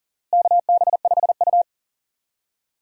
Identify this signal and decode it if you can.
Morse code, heard as a keyed tone, about 40 wpm, 700 Hz. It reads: KB5U
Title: USA Call Signs - 4 Characters - Repeated 3x - 40wpm